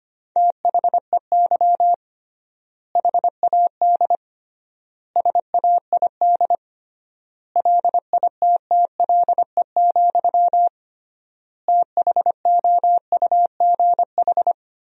Morse code, heard as a keyed tone, about 25 wpm, 700 Hz. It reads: THEY HAD SAID LITTLE, THOUGH
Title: 3x Word-Spacing - Sentences from Top 400 Words - 25wpm